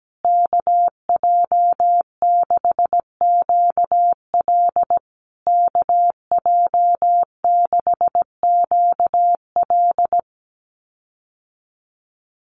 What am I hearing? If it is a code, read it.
Morse code, heard as a keyed tone, about 17 wpm, 700 Hz. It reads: KJ6QL KJ6QL
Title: USA Call Signs Repeated 3x 17wpm